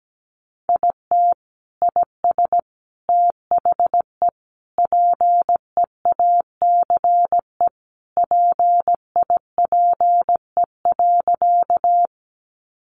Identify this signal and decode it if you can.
Morse code, heard as a keyed tone, about 17 wpm, 700 Hz. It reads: IT IS THE PEACE PIPE.
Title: Sentences from Top 1500 Words 17wpm